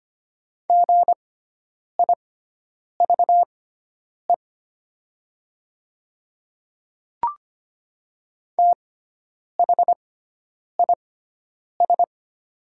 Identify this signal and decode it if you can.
Morse code, heard as a keyed tone, about 25 wpm, spaced as for 9 wpm, 700 Hz. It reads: GIVE THIS
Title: 6x Character Spacing - Top 200 Words - 25wpm